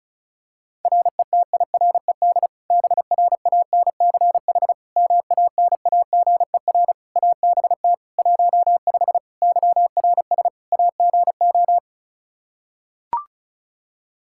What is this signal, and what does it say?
Morse code, heard as a keyed tone, about 34 wpm, 700 Hz. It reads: RETIRED BRANCH MANAGER ABT 15 YRS AGO
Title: Practice QSO - Exchange 3 - More Info - 35wpm